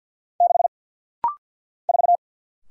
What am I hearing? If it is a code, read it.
Morse code, heard as a keyed tone, about 48 wpm, 700 Hz. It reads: 6 4
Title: Single Letter-Number - Rapid-Fire 50wpm